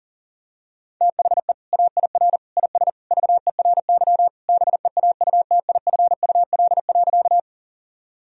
Morse code, heard as a keyed tone, about 40 wpm, 700 Hz. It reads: THE AIR IS VERY BEAUTIFUL.
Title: Sentences from Top 1000 Words - Review Part 1 - 40wpm